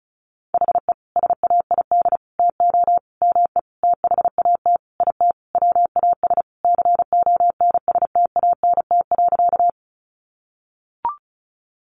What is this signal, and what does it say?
Morse code, heard as a keyed tone, about 34 wpm, 700 Hz. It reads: HE SAID TO ME THAT IT WAS CONSTANT.
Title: Sentences from Top 1500 Words 35wpm